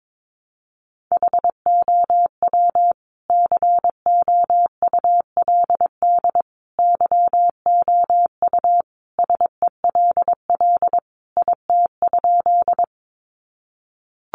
Morse code, heard as a keyed tone, about 22 wpm, 700 Hz. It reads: HOW COULD YOU SELL IT?
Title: Sentences from Top 1100 Words 22wpm